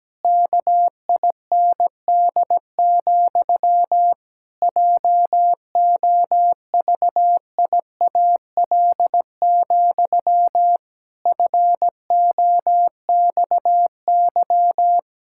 Morse code, heard as a keyed tone, about 17 wpm, 700 Hz. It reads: KIND, JOVIAL, FOXY
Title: Pangram Challenge v2 - 17wpm